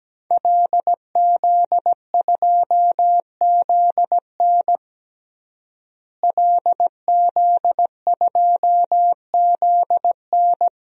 Morse code, heard as a keyed tone, about 17 wpm, 700 Hz. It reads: LZ2ZN LZ2ZN
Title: DXCC-100 Call Signs - 5 Characters - Repeated 3x - 17wpm